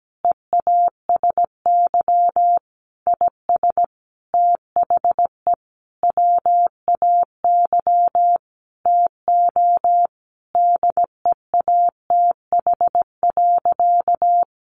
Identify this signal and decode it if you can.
Morse code, heard as a keyed tone, about 17 wpm, 700 Hz. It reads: EASY IS THE WAY TO DEATH.
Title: Sentences from Top 1400 Words 17wpm